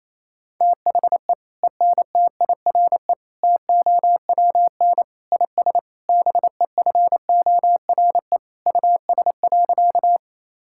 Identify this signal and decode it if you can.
Morse code, heard as a keyed tone, about 28 wpm, 700 Hz. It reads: THE ENTIRE TOWN IS BEFORE US.
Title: Sentences from Top 1000 Words - Review Part 5 - 28wpm